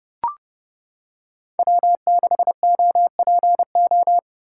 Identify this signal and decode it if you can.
Morse code, heard as a keyed tone, about 30 wpm, 700 Hz. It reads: W6OPO
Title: K1USN SST QSO Practice - 30wpm